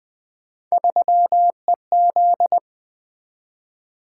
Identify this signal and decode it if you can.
Morse code, heard as a keyed tone, about 20 wpm, 700 Hz. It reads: 3EZ
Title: Three Letter-Number Combinations - 20wpm